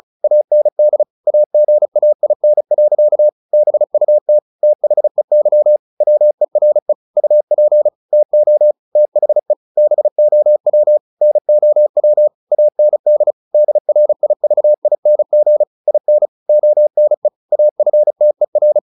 Morse code, heard as a keyed tone, about 34 wpm, 600 Hz. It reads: AND AGAIN. BUT THEY WERE UP TO THE BOW NOW AND DRIVING IN ONE AFTER
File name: the old man and the sea 2 35wpm